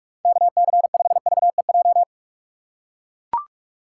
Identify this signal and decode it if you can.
Morse code, heard as a keyed tone, about 42 wpm, 700 Hz. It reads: KC5VEJ
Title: USA Call Signs - 4x Character Spacing - 45wpm